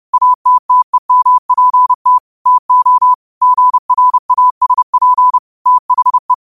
Morse code, heard as a keyed tone, about 30 wpm, 1000 Hz. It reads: ATTEMPT TO GRASP THE